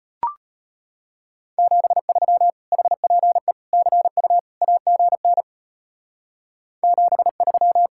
Morse code, heard as a keyed tone, about 30 wpm, 700 Hz. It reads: 73 HPE CU AGN 73
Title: Speed-Racing Common QSO Phrases Encore - 25wpm